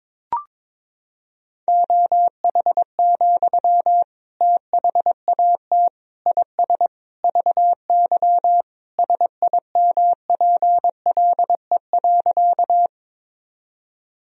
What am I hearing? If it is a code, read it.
Morse code, heard as a keyed tone, about 22 wpm, 700 Hz. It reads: OH, THAT IS VY SIMPLE.
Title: Sentences from Top 500 Words and Abbreviations 22wpm